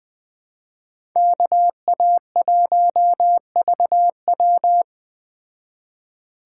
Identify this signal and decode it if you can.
Morse code, heard as a keyed tone, about 20 wpm, 700 Hz. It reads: KA1VW
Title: Speed Racing - USA 2x2 Call Signs 17wpm